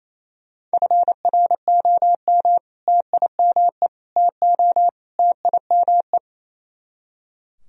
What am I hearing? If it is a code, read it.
Morse code, heard as a keyed tone, about 34 wpm, 700 Hz. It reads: FROM TIME TO TIME
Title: Speed-Racing - Sets of 4 Words 28wpm